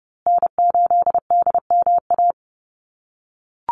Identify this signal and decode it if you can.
Morse code, heard as a keyed tone, about 30 wpm, 700 Hz. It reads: N8DMA
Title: USA Call Signs Repeated 3x 30wpm